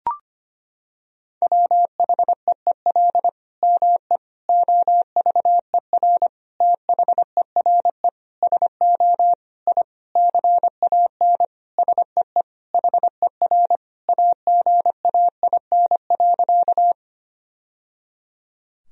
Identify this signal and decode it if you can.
Morse code, heard as a keyed tone, about 25 wpm, 700 Hz. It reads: WHEEL ME OVER THERE SO I CAN SEE HER AGAIN.
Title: Sentences from Top 1000 Words - Review Part 3 - 25wpm